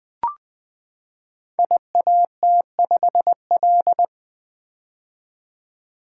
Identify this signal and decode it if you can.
Morse code, heard as a keyed tone, about 20 wpm, 700 Hz. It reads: IAT5L
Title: Code Groups - 5 Letter:Numbers - 20wpm